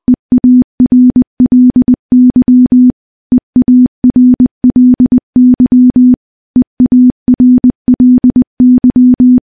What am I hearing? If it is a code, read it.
Morse code, heard as a keyed tone, about 20 wpm, 250 Hz. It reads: EARLY EARLY EARLY
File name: morse / morse-training-20WPM